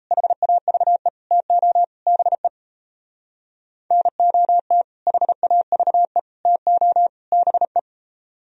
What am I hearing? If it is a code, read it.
Morse code, heard as a keyed tone, about 35 wpm, 700 Hz. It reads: HAVE TO BE NOT HAVE TO BE